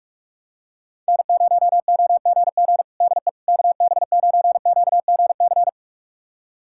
Morse code, heard as a keyed tone, about 46 wpm, 700 Hz. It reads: N0OGG DE KB9QGC
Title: USA Callsign DE Callsign 45wpm